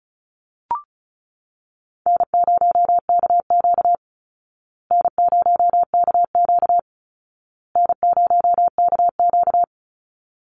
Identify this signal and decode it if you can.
Morse code, heard as a keyed tone, about 34 wpm, 700 Hz. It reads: N0KQ N0KQ N0KQ
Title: Easy Call Signs Repeated 3x - 35wpm